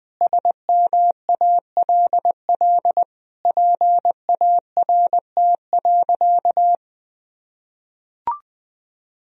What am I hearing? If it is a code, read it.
Morse code, heard as a keyed tone, about 21 wpm, 700 Hz. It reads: SMALL PART.
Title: Sentences from Top 1000 Words - Review Part 1 - 20wpm